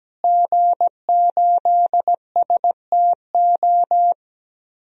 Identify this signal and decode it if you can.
Morse code, heard as a keyed tone, about 17 wpm, 700 Hz. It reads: G8STO
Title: DXCC-100 Call Signs - 5 Characters - Repeated 3x - 17wpm